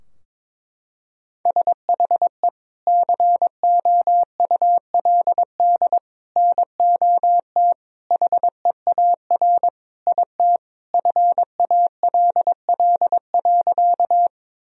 Morse code, heard as a keyed tone, about 22 wpm, 700 Hz. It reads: SHE COULD NOT HEAR IT FALL.
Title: Sentences from Top 1000 Words - Review Part 2 - 22wpm